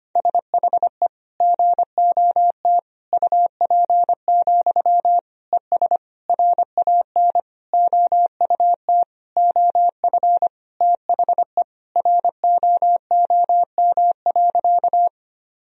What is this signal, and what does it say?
Morse code, heard as a keyed tone, about 25 wpm, 700 Hz. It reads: SHE GOT UP, ES RAN OUT OF THE ROOM.
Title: Sentences from Top 500 Words and Abbreviations 25wpm